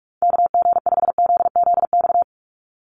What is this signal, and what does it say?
Morse code, heard as a keyed tone, about 44 wpm, 700 Hz. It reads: KG5ZZX
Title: USA Call Signs 45wpm